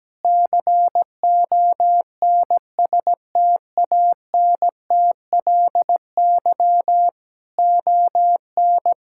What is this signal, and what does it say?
Morse code, heard as a keyed tone, about 17 wpm, 700 Hz. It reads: CONSTANTLY ON
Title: Sentences from Top 1600 Words 17wpm